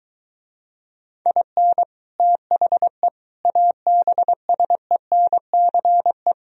Morse code, heard as a keyed tone, about 24 wpm, 700 Hz. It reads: IN THE ABSENCE